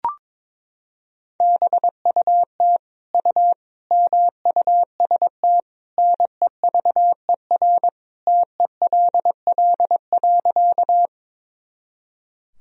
Morse code, heard as a keyed tone, about 22 wpm, 700 Hz. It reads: BUT U MUST NEVER TELL.
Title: Sentences from Top 300 Words and Abbreviations 22wpm